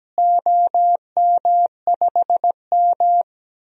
Morse code, heard as a keyed tone, about 17 wpm, 700 Hz. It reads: OM5M
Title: DXCC-100 Call Signs - 4 Characters - Repeated 2x - 17wpm